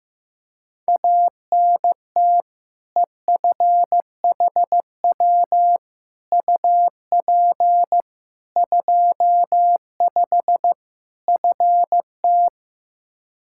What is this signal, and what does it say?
Morse code, heard as a keyed tone, about 15 wpm, 700 Hz. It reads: ANT EFHW UP 25 FT
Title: K1USN SST QSO Practice 15wpm